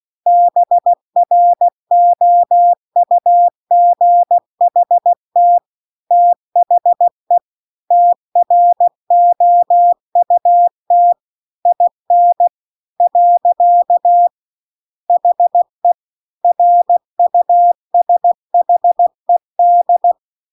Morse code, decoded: BROUGHT THE TROUT IN . HE RUSHED

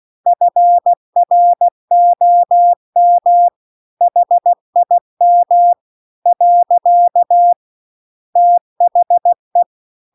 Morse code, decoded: FROM HIM . THE

